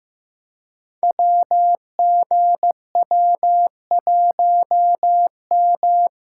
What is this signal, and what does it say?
Morse code, heard as a keyed tone, about 15 wpm, 700 Hz. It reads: WGW1M
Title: Code Groups - 5 Letter:Numbers - 15wpm